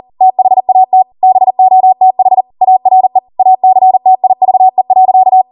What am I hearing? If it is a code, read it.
Morse code, heard as a keyed tone, about 40 wpm, 750 Hz. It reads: THAT BOTH ARE ACTIVE.